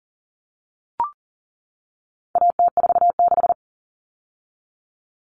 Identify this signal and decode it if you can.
Morse code, heard as a keyed tone, about 40 wpm, 700 Hz. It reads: AT46
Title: All Cut Numbers - 4-Digit 40wpm